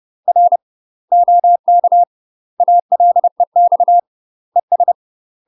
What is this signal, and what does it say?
Morse code, heard as a keyed tone, about 30 wpm, 700 Hz. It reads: R OK ALEX ES
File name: UK MASTER QSOs TRAINING / 30WPM / UK MASTER QSOs 0032